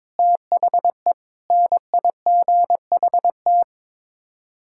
Morse code, heard as a keyed tone, about 22 wpm, 700 Hz. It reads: THE NIGHT
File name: practice / 22wpm / Sets of 8 Words - 22wpm